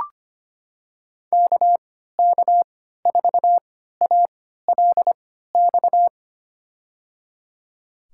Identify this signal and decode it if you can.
Morse code, heard as a keyed tone, about 25 wpm, spaced as for 14 wpm, 700 Hz. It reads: KK4ALX
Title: USA Call Signs - 6 Characters - 3x Character Spacing - 25wpm